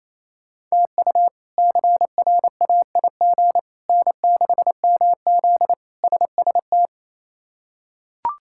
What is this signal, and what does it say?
Morse code, heard as a keyed tone, about 28 wpm, 700 Hz. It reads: TU CRAIG N6MZ SST
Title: K1USN SST QSO Practice - 28wpm